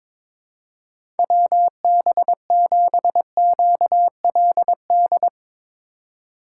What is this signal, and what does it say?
Morse code, heard as a keyed tone, about 21 wpm, 700 Hz. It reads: WB7QLD
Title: USA Call Signs - 4x Character Spacing - 22wpm